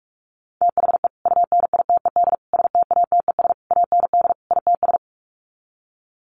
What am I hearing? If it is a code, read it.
Morse code, heard as a keyed tone, about 44 wpm, 700 Hz. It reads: THE UNITED STATES AND ITS
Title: Sets of 5 Words Encore 45wpm